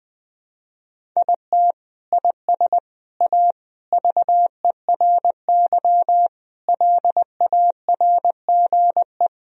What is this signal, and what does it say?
Morse code, heard as a keyed tone, about 20 wpm, 700 Hz. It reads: IT IS A VERY LARGE
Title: Sentences from Top 1000 Words - Review Part 1 - 20wpm